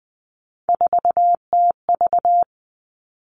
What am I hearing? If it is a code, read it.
Morse code, heard as a keyed tone, about 20 wpm, 700 Hz. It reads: 4TV
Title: Three Challenging Letter-Number Combinations 20wpm